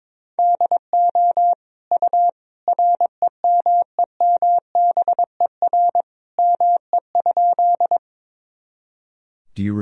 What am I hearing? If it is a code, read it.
Morse code, heard as a keyed tone, about 22 wpm, 700 Hz. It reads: DO U REMEMBER ME?
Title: Sentences from Top 400 Words and Abbreviations 22wpm